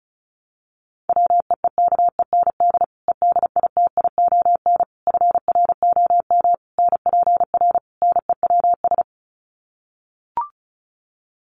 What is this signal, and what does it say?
Morse code, heard as a keyed tone, about 34 wpm, 700 Hz. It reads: WEEKEND EDITION FROM NPR NEWS
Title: Sets of 5 Words 35wpm